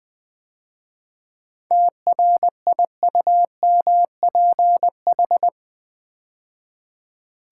Morse code, heard as a keyed tone, about 20 wpm, 700 Hz. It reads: TRIUMPH
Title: 7 Letter Words - First 3 Letters Given - 20wpm